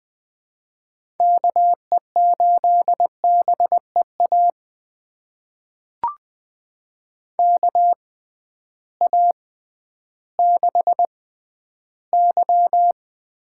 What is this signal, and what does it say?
Morse code, heard as a keyed tone, about 20 wpm, spaced as for 12 wpm, 700 Hz. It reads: KE8BEA KA6Y
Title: USA Call Signs - 6 Characters - 6x Character Spacing - 20wpm